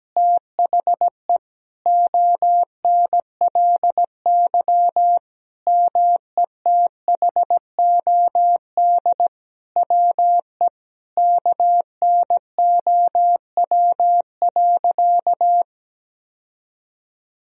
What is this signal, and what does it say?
Morse code, heard as a keyed tone, about 17 wpm, 700 Hz. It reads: THE ONLY METHOD WE KNOW.
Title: Sentences from Top 1000 Words - Review Part 3 - 17wpm